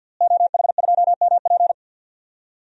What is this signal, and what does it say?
Morse code, heard as a keyed tone, about 53 wpm, 700 Hz. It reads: OH2MP
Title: DXCC-100 Call Signs - 5 Characters - Repeated 2x - 50wpm